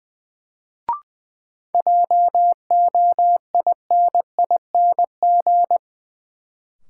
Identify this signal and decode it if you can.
Morse code, heard as a keyed tone, about 21 wpm, 700 Hz. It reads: JOINING
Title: Sets of 5 Words - New Word Drill - 20wpm